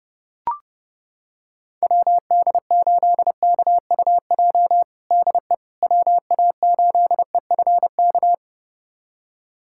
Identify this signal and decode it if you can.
Morse code, heard as a keyed tone, about 32 wpm, 700 Hz. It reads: WD8KUJ DE WA8EFK
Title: USA Callsign DE Callsign 30wpm